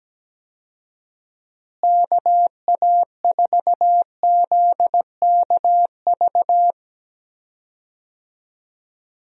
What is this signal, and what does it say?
Morse code, heard as a keyed tone, about 17 wpm, 700 Hz. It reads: KA4ZKV
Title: USA Call Signs - 6 Characters - Repeated 3x - 17wpm